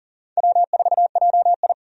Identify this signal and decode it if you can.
Morse code, heard as a keyed tone, about 40 wpm, 700 Hz. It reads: W4JI
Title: USA Call Signs - 4 Characters - 40wpm